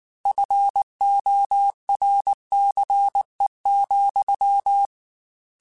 Morse code, heard as a keyed tone, about 19 wpm, 800 Hz. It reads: FORCE,